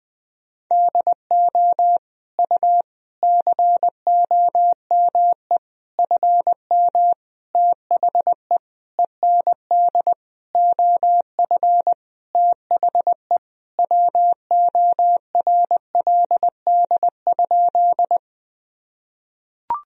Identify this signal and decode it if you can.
Morse code, heard as a keyed tone, about 21 wpm, 700 Hz. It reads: DO U COME FM THE END OF THE WORLD?
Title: Sentences from Top 200 Words and Abbreviations 20wpm